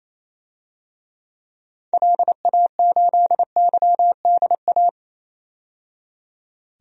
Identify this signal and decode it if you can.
Morse code, heard as a keyed tone, about 28 wpm, 700 Hz. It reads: LA8YDA
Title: DXCC-100 Call Signs - 6 Characters - 28wpm